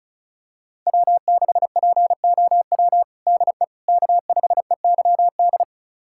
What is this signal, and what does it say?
Morse code, heard as a keyed tone, about 36 wpm, 700 Hz. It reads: W6POW DE K5EYD
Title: USA Callsign DE Callsign 35wpm